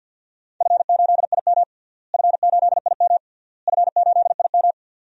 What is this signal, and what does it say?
Morse code, heard as a keyed tone, about 50 wpm, 700 Hz. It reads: F8IM F8IM F8IM
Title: DXCC-100 Call Signs - 4 Characters - Repeated 3x - 50wpm